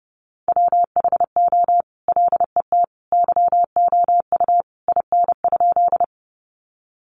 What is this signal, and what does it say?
Morse code, heard as a keyed tone, about 30 wpm, 700 Hz. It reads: WHO LET YOU IN?
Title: Sentences from Top 300 Words 30wpm